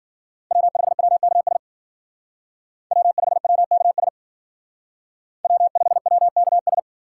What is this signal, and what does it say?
Morse code, heard as a keyed tone, about 51 wpm, 700 Hz. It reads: W5WKS W5WKS W5WKS